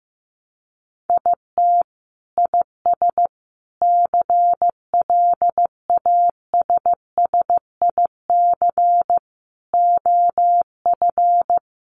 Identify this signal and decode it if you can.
Morse code, heard as a keyed tone, about 15 wpm, 700 Hz. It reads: IT IS CLASSIC OF
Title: Sentences from Top 1000 Words 15wpm